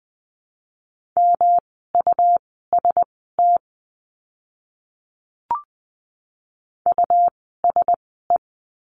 Morse code, heard as a keyed tone, about 20 wpm, spaced as for 14 wpm, 700 Hz. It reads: MUST USE
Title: 237 Lesson 38f - Letter q - Words 20wpm